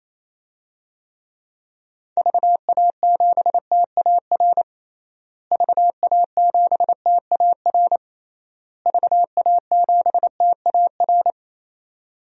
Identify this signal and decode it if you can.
Morse code, heard as a keyed tone, about 28 wpm, 700 Hz. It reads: VA7TAR VA7TAR VA7TAR
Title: DXCC-100 Call Signs - 6 Characters - Repeated 3x - 28wpm